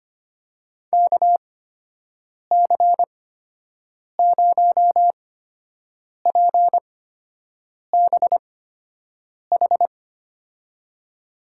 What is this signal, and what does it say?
Morse code, heard as a keyed tone, about 25 wpm, spaced as for 7 wpm, 700 Hz. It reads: KC0PBH
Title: USA Call Signs - 6 Characters - 8x Character Spacing - 25wpm